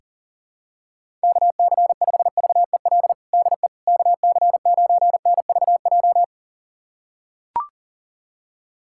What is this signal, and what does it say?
Morse code, heard as a keyed tone, about 40 wpm, 700 Hz. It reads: KC5VEL DE KC9NVJ
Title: USA Callsign DE Callsign - 6 Characters - 40wpm